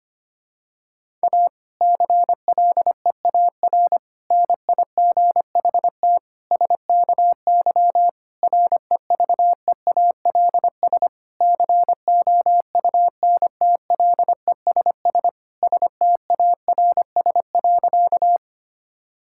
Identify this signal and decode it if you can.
Morse code, heard as a keyed tone, about 25 wpm, 700 Hz. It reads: A CLEAR NIGHT SKY REVEALS COUNTLESS STARS.
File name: practice / 25wpm / Difficult Sentences with Hints - 25wpm